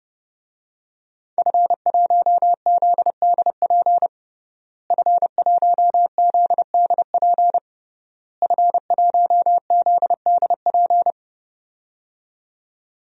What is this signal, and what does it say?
Morse code, heard as a keyed tone, about 28 wpm, 700 Hz. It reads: F1ZDP F1ZDP F1ZDP
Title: DXCC-100 Call Signs - 5 Characters - Repeated 3x - 30wpm